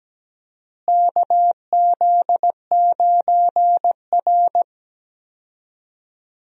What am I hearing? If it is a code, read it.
Morse code, heard as a keyed tone, about 17 wpm, 700 Hz. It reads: KZ9R
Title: USA Call Signs - 4 Characters - Repeated 3x - 17wpm